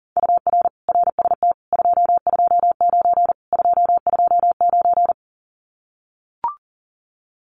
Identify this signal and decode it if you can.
Morse code, heard as a keyed tone, about 38 wpm, 700 Hz. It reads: UR RST 229 229
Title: Common QSO Phrases Encore 40wpm